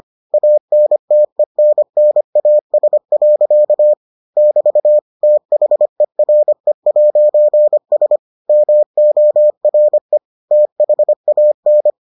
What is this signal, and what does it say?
Morse code, heard as a keyed tone, about 25 wpm, 600 Hz. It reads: ANTENNAS. <BT> THERE'S MORE THAN